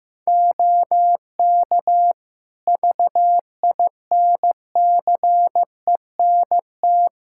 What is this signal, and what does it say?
Morse code, heard as a keyed tone, about 15 wpm, 700 Hz. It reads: OK VINCENT
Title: Practice QSO - Exchange 3 - More Info - 15wpm